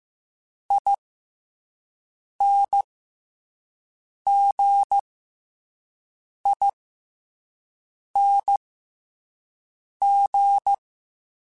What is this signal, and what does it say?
Morse code, heard as a keyed tone, about 15 wpm, spaced as for 5 wpm, 800 Hz. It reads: INGING